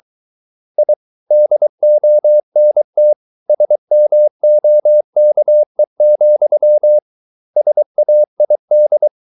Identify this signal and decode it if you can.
Morse code, heard as a keyed tone, about 23 wpm, 600 Hz. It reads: I DONT SMOKE, SAID